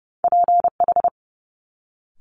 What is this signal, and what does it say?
Morse code, heard as a keyed tone, about 30 wpm, 700 Hz. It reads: PH
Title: Multi-letter Phonemes Quick Fire 30wpm